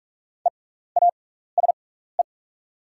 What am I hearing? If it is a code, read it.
Morse code, heard as a keyed tone, about 44 wpm, spaced as for 15 wpm, 690 Hz. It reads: EASE
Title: 6x Character Spacing - Top 500 Words - 45wpm